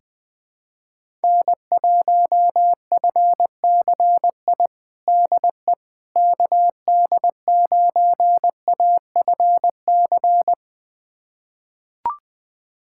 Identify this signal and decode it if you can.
Morse code, heard as a keyed tone, about 21 wpm, 700 Hz. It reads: N1FCI DE KD9AFC
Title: USA Callsign DE Callsign 20wpm